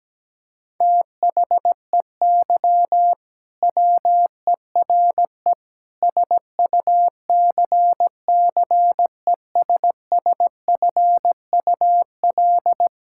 Morse code, heard as a keyed tone, about 17 wpm, 700 Hz. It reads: THEY WERE SUCCESSFUL
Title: Sentences from Top 1000 Words - Review Part 6 - 17wpm